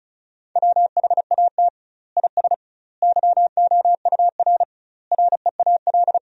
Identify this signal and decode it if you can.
Morse code, heard as a keyed tone, about 36 wpm, 700 Hz. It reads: WHAT IS YOUR REAL
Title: 2x Word-Spacing - Sentences from Top 300 Words - 35wpm